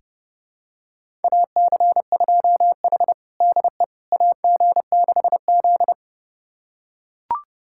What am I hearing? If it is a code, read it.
Morse code, heard as a keyed tone, about 32 wpm, 700 Hz. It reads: AC2H DE AG6Z
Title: USA Callsign DE Callsign - 4 Characters - 30wpm